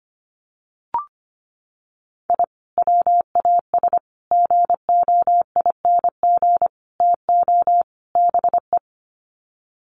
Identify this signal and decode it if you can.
Morse code, heard as a keyed tone, about 25 wpm, 700 Hz. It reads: I WAS GOING TO BE
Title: Sets of 5 Words 25wpm